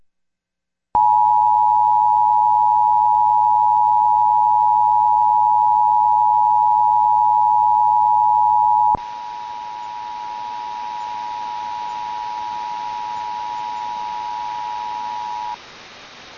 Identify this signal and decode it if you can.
Emergency Alert System (SAME) tones (the 853 and 960 Hz attention tone). It is heard from 1.0 to 15.6 s.